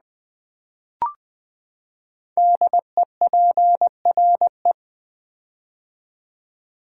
Morse code, heard as a keyed tone, about 20 wpm, 700 Hz. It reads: DEPRE
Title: Build A Word - 5 Letters At A Time - 10-Letter Words - 20wpm